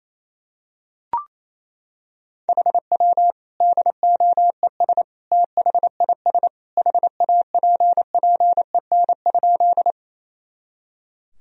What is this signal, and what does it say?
Morse code, heard as a keyed tone, about 28 wpm, 700 Hz. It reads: HW DOES THIS HAPPEN?
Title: Sentences from Top 400 Words and Abbreviations 28wpm